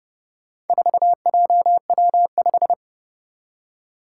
Morse code, heard as a keyed tone, about 30 wpm, 700 Hz. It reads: 4JW5
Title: Four Challenging Letter-Number Combinations 30wpm